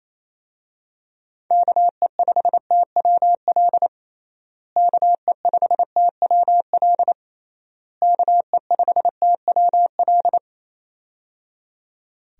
Morse code, heard as a keyed tone, about 28 wpm, 700 Hz. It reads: KE5TWL KE5TWL KE5TWL
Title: USA Call Signs - 6 Characters - Repeated 3x - 28wpm